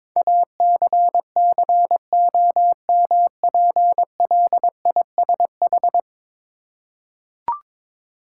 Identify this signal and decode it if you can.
Morse code, heard as a keyed tone, about 22 wpm, 700 Hz. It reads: ACCOMPLISH
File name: practice / 22wpm / 10 Letter Words - First 3 Letters Given - 22wpm